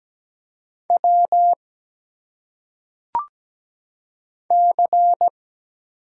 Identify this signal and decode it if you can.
Morse code, heard as a keyed tone, about 17 wpm, 700 Hz. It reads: W C
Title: Single Letters - Phonetic Pronunciation 17wpm